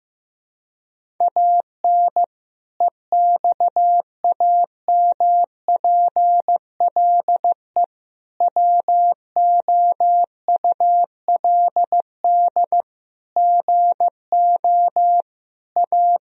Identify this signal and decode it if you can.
Morse code, heard as a keyed tone, about 15 wpm, 700 Hz. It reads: AN EXAMPLE WOULD GO A
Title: Sentences from Top 1000 Words - Review Part 1 - 15wpm